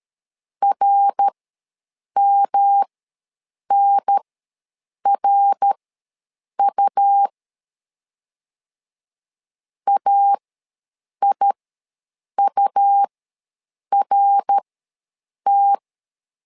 Morse code, heard as a keyed tone, about 13 wpm, spaced as for 7 wpm, 800 Hz. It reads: RMNRU AIURT
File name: LEZIONI / 3 MWRO / 3 MWR ws 12wpm 1809